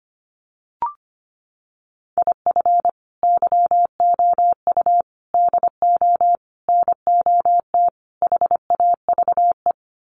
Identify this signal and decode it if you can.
Morse code, heard as a keyed tone, about 25 wpm, 700 Hz. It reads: IF YOU DO NOT HAVE
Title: Sets of 5 Words 25wpm